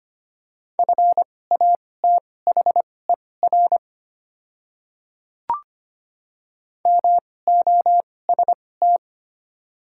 Morse code, heard as a keyed tone, about 25 wpm, spaced as for 18 wpm, 700 Hz. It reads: FATHER MOST
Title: 2x Character Spacing - Top 300 Words - 25wpm